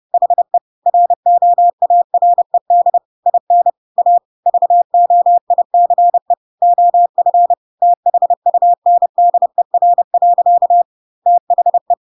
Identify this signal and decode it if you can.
Morse code, heard as a keyed tone, about 30 wpm, 700 Hz. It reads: HE ROARED IN A VOICE OF THUNDER. THE
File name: Pinocchio / Pinocchio ch3 30 30 ik1hge